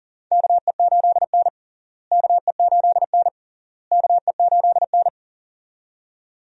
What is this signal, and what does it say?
Morse code, heard as a keyed tone, about 37 wpm, 700 Hz. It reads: KE8N KE8N KE8N
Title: USA Call Signs - 4 Characters - Repeated 3x - 40wpm